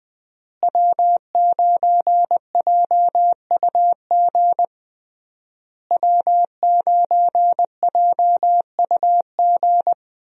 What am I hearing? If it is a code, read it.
Morse code, heard as a keyed tone, about 21 wpm, 700 Hz. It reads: W9JUG W9JUG
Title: USA Call Signs - 5 Characters - Repeated 3x - 20wpm